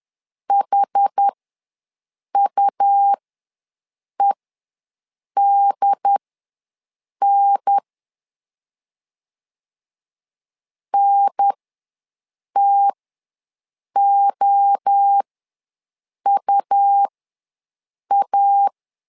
Morse code, heard as a keyed tone, about 10 wpm, spaced as for 6 wpm, 800 Hz. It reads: HUEDN NTOUA
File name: LEZIONI / 3 MWRO / 3 MWRO ws 10wpm 2310